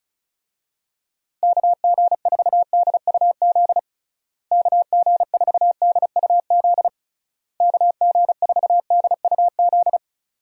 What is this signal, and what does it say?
Morse code, heard as a keyed tone, about 34 wpm, 700 Hz. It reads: KG4DUZ KG4DUZ KG4DUZ
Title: USA Call Signs - 6 Characters - Repeated 3x - 35wpm